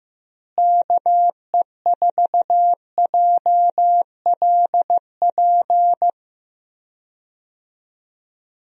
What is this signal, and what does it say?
Morse code, heard as a keyed tone, about 15 wpm, 700 Hz. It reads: KE4JLP
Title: USA Call Signs - 6 Characters - Repeated 3x - 15wpm